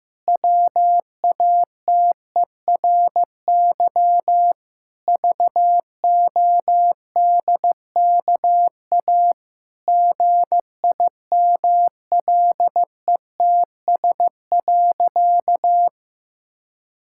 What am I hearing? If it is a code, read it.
Morse code, heard as a keyed tone, about 15 wpm, 700 Hz. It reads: WATERY VODKA GIMLETS.
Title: Pangram Challenge v2 - 15wpm